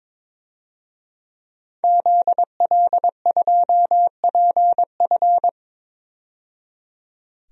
Morse code, heard as a keyed tone, about 21 wpm, 700 Hz. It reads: ZL2PF
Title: DXCC-100 Call Signs - 5 Characters - 22wpm